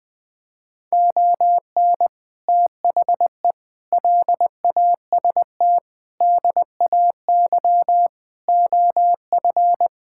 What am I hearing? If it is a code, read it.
Morse code, heard as a keyed tone, about 20 wpm, 700 Hz. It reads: ON THE LAST DAY OF
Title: Sets of 6 Words - 20wpm